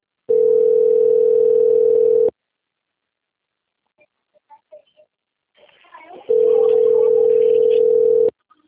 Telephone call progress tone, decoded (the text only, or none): ringback tone